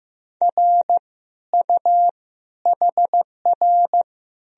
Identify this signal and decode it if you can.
Morse code, heard as a keyed tone, about 15 wpm, 700 Hz. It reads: R U HR
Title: Sentences from Top 400 Words and Abbreviations 15wpm